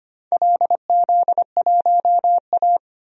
Morse code, heard as a keyed tone, about 25 wpm, 690 Hz. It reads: LZ1A
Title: DXCC-100 Call Signs - 4 Characters - 25wpm